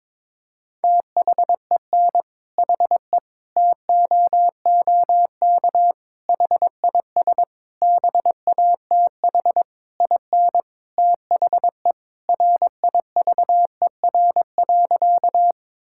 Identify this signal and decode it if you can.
Morse code, heard as a keyed tone, about 22 wpm, 700 Hz. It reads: THEN HE TOOK HIS BATH IN THE RIVER.
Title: Sentences from Top 1600 Words 22wpm